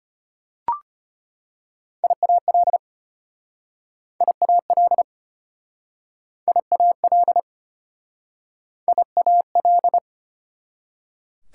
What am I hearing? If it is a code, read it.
Morse code, heard as a keyed tone, about 33 wpm, 700 Hz. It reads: IAL IAL IAL IAL